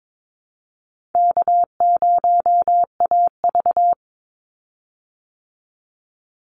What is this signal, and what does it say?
Morse code, heard as a keyed tone, about 22 wpm, 700 Hz. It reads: K0AV
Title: Easy Call Signs Repeated 2x - 22wpm